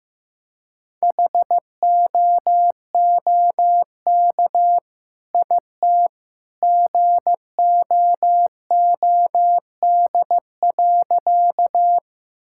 Morse code, heard as a keyed tone, about 15 wpm, 700 Hz. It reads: HOOK IT GOOD.
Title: Sentences from Top 1600 Words 15wpm